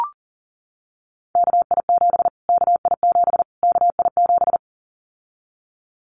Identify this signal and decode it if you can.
Morse code, heard as a keyed tone, about 40 wpm, 700 Hz. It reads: KI7 KI7 KI7
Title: Callsign Prefixes for Top 25 DXCC - Repeated 3x - 40wpm